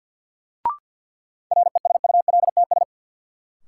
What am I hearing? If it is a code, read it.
Morse code, heard as a keyed tone, about 50 wpm, 700 Hz. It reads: RESULTS
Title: Sets of 4 Words - New Word Drill - 50wpm